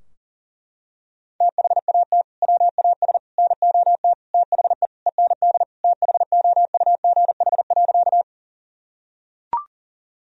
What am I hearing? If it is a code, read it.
Morse code, heard as a keyed tone, about 40 wpm, 700 Hz. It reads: THAT WAS NOT THE END THOUGH.
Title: Sentences from Top 1000 Words - Review Part 2 - 40wpm